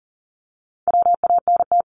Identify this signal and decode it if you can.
Morse code, heard as a keyed tone, about 43 wpm, 700 Hz. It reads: WANT
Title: Multi-letter Phonemes 40wpm